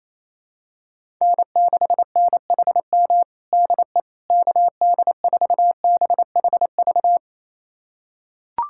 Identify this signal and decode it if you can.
Morse code, heard as a keyed tone, about 28 wpm, 700 Hz. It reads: N6NHM DE KD4BHV
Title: USA Callsign DE Callsign 28wpm